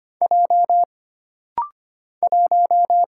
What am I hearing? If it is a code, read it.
Morse code, heard as a keyed tone, about 25 wpm, 700 Hz. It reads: J 1
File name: Single Character (Letter:Number:Punctuation:Prosign) - Rapid-Fire - 25wpm